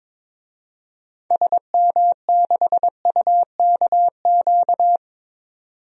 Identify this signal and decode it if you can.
Morse code, heard as a keyed tone, about 22 wpm, 700 Hz. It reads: SM6UKQ
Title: DXCC-100 Call Signs - 6 Characters - 22wpm